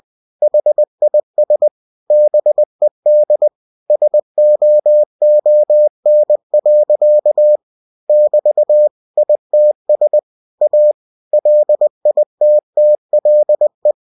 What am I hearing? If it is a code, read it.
Morse code, heard as a keyed tone, about 20 wpm, 600 Hz. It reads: HIS BED SOON. <BT> ITS A LITTLE
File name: cw-books / winnie-the-pooh / winnie-20wpm-page-0091